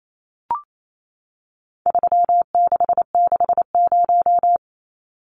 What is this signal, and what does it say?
Morse code, heard as a keyed tone, about 28 wpm, 700 Hz. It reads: 3660
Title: Numbers - 4-Digit 28wpm